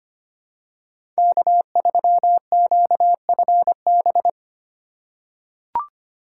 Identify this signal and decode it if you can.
Morse code, heard as a keyed tone, about 25 wpm, 700 Hz. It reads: K3QFB
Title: USA Call Signs - 3x Character Spacing - 25wpm